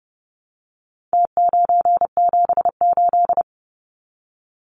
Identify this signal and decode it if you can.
Morse code, heard as a keyed tone, about 30 wpm, 700 Hz. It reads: T978
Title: Numbers - Contest Style - 4-Digit 30wpm